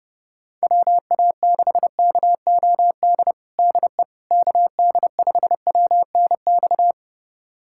Morse code, heard as a keyed tone, about 30 wpm, 700 Hz. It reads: WA6KOD DE KD5WNX
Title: USA Callsign DE Callsign - 6 Characters - 30wpm